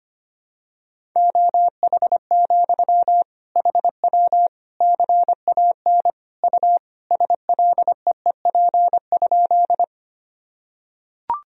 Morse code, heard as a keyed tone, about 25 wpm, 700 Hz. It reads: OH, HW CAN U SLEEP?
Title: Sentences from Top 500 Words and Abbreviations 25wpm